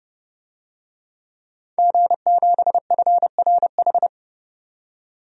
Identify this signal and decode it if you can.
Morse code, heard as a keyed tone, about 32 wpm, 700 Hz. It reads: G7FRH
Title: DXCC-100 Call Signs - 5 Characters - 30wpm